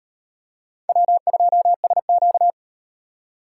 Morse code, heard as a keyed tone, about 35 wpm, 700 Hz. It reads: W2SQ